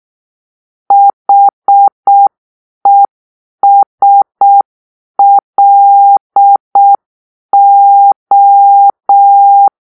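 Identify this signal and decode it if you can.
Morse code, heard as a keyed tone, about 6 wpm, 800 Hz. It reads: HESLO
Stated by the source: Morse-code cipher made by a player team for the Kachna puzzle game.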